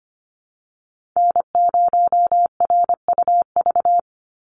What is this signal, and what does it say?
Morse code, heard as a keyed tone, about 25 wpm, 700 Hz. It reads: N0RUV
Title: USA Call Signs 25wpm